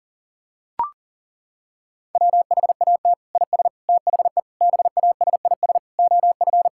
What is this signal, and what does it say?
Morse code, heard as a keyed tone, about 40 wpm, 700 Hz. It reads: WHAT IS THE BASIS OF
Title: Sentences from Top 1000 Words - Review Part 5 - 40wpm